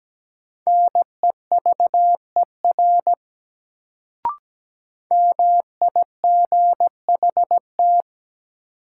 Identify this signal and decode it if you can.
Morse code, heard as a keyed tone, about 17 wpm, 700 Hz. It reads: NEVER MIGHT
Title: Sentences from Top 300 Words - New Word Drill - 17wpm